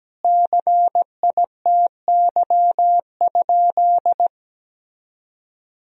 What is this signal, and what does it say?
Morse code, heard as a keyed tone, about 17 wpm, 700 Hz. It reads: CITY?
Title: Sentences from Top 1000 Words - Review Part 2 - 17wpm